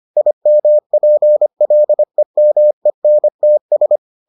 Morse code, heard as a keyed tone, about 25 wpm, 600 Hz. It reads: IMPLEMENTS